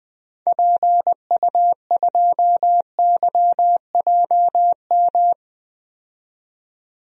Morse code, heard as a keyed tone, about 21 wpm, 700 Hz. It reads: PU2YJM
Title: DXCC-100 Call Signs - 6 Characters - 20wpm